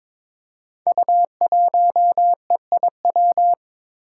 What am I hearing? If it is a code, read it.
Morse code, heard as a keyed tone, about 22 wpm, 700 Hz. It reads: U1EIW
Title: Five Challenging Letter-Number Combinations 22wpm